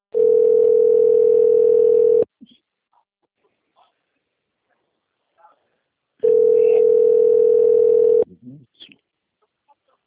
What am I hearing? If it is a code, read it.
Ringback tone (North America), 440 + 480 Hz.
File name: rg-1600-unknown- - - .66983